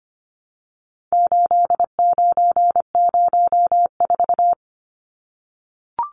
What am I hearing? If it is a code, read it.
Morse code, heard as a keyed tone, about 25 wpm, 700 Hz. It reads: 8904
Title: Numbers - 4-Digit 25wpm